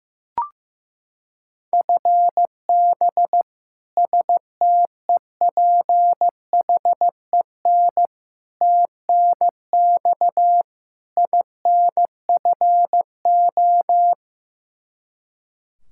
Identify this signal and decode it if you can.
Morse code, heard as a keyed tone, about 15 wpm, 700 Hz. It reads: FB STEPHEN TNX INFO
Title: Practice QSO - Exchange 3 - More Info - 15wpm